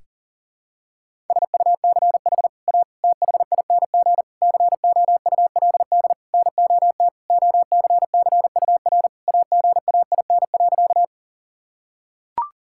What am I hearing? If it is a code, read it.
Morse code, heard as a keyed tone, about 40 wpm, 700 Hz. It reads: SUCH A THING COULD NOT OCCUR AGAIN.
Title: Sentences from Top 1000 Words - Review Part 6 - 40wpm